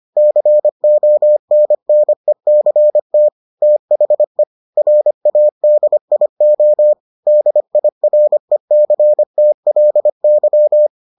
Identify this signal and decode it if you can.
Morse code, heard as a keyed tone, about 25 wpm, 600 Hz. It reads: CONNECT THE RADIO DIRECTLY